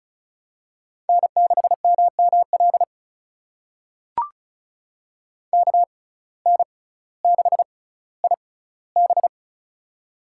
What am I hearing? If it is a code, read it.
Morse code, heard as a keyed tone, about 36 wpm, spaced as for 18 wpm, 700 Hz. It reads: N6MML KN6IB
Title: USA Call Signs - 5 Characters - 6x Character Spacing - 35wpm